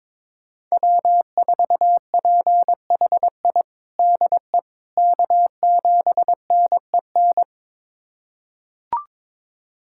Morse code, heard as a keyed tone, about 22 wpm, 700 Hz. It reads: W4PHI DE K7NEN
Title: USA Callsign DE Callsign - 5 Characters - 22wpm